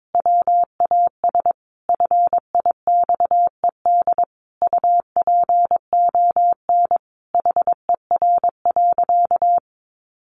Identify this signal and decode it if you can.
Morse code, heard as a keyed tone, about 22 wpm, 700 Hz. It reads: WAS FIXED UPON HER.
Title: Sentences from Top 1100 Words 22wpm